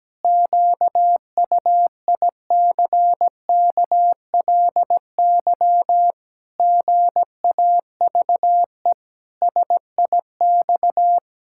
Morse code, heard as a keyed tone, about 17 wpm, 700 Hz. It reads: QUICKLY GAVE SIX
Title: Pangram Challenge v2 - 17wpm